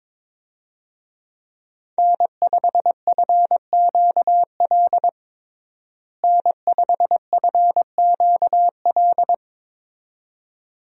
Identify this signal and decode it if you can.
Morse code, heard as a keyed tone, about 22 wpm, 700 Hz. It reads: N5FQL N5FQL
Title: USA Call Signs - 5 Characters - Repeated 2x - 22wpm